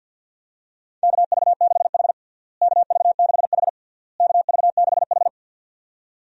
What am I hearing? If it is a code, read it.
Morse code, heard as a keyed tone, about 53 wpm, 700 Hz. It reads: KV6H KV6H KV6H
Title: USA Call Signs - 4 Characters - Repeated 3x - 50wpm